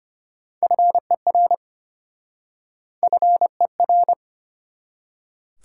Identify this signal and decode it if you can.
Morse code, heard as a keyed tone, about 33 wpm, 700 Hz. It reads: FER FER